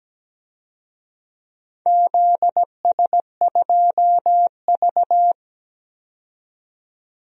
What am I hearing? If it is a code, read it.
Morse code, heard as a keyed tone, about 17 wpm, 700 Hz. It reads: ZS2V
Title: DXCC-100 Call Signs - 4 Characters - 17wpm